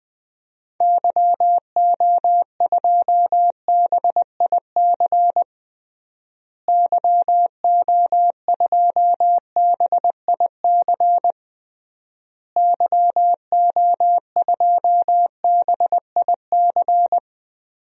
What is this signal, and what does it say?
Morse code, heard as a keyed tone, about 21 wpm, 700 Hz. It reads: YO2BIC YO2BIC YO2BIC
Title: DXCC-100 Call Signs - 6 Characters - Repeated 3x - 20wpm